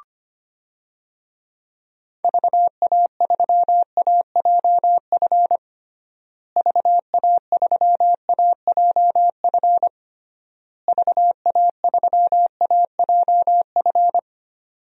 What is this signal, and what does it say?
Morse code, heard as a keyed tone, about 25 wpm, 700 Hz. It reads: VA3AJF VA3AJF VA3AJF
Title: DXCC-100 Call Signs - 6 Characters - Repeated 3x - 25wpm